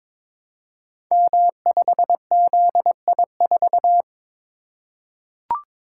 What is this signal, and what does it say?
Morse code, heard as a keyed tone, about 22 wpm, 700 Hz. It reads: M5ZI4
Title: Code Groups - 5 Letter:Numbers - 22wpm